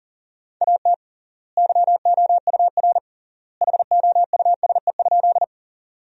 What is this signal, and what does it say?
Morse code, heard as a keyed tone, about 40 wpm, 700 Hz. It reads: AT YOUR HOUSE?
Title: 3x Word-Spacing - Sentences from Top 200 Words - 40wpm